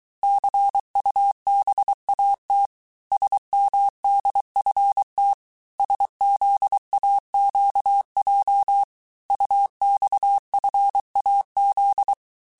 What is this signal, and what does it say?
Morse code, heard as a keyed tone, about 23 wpm, 800 Hz. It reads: CUBAT SMDFT SZAQJ UXFAZ
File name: cours-du-mercredi / audios / cours-855-texte-chiffre-vitesse-22-mpm